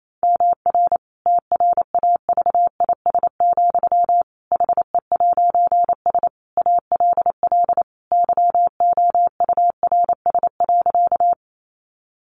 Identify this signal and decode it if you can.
Morse code, heard as a keyed tone, about 28 wpm, 700 Hz. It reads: MR TRAVIS, HE'S ALL YOURS.
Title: A Sound Of Thunder 28wpm